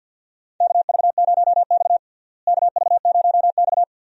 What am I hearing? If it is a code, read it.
Morse code, heard as a keyed tone, about 51 wpm, 700 Hz. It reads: KV0X KV0X
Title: USA Call Signs - 4 Characters - Repeated 2x - 50wpm